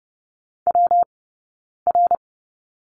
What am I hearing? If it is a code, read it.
Morse code, heard as a keyed tone, about 30 wpm, spaced as for 9 wpm, 690 Hz. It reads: WR
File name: Lesson 19d - Letter w - Two Characters 30wpm